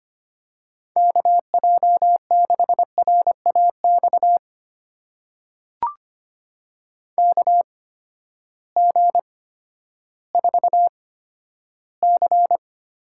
Morse code, heard as a keyed tone, about 25 wpm, spaced as for 12 wpm, 700 Hz. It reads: KJ6RAX KG4C
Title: USA Call Signs - 6 Characters - 8x Character Spacing - 25wpm